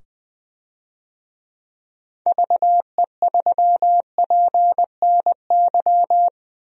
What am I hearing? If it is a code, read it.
Morse code, heard as a keyed tone, about 20 wpm, 700 Hz. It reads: VE3PNY
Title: DXCC-100 Call Signs - 6 Characters - 20wpm